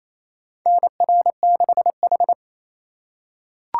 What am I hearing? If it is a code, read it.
Morse code, heard as a keyed tone, about 28 wpm, 700 Hz. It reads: NR6H
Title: USA Call Signs - 4 Characters - 2x Character Spacing - 28wpm